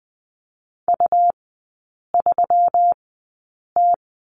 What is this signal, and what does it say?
Morse code, heard as a keyed tone, about 20 wpm, spaced as for 8 wpm, 690 Hz. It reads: U3T
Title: 198 Lesson 32e - Letter k - Three Characters 20wpm